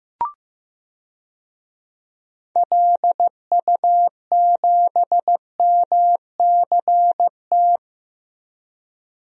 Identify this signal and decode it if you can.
Morse code, heard as a keyed tone, about 15 wpm, 700 Hz. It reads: LU7MCT
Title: DXCC-100 Call Signs - 6 Characters - 15wpm